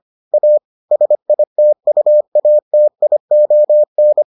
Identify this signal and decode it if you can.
Morse code, heard as a keyed tone, about 25 wpm, 600 Hz. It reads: A SITUATION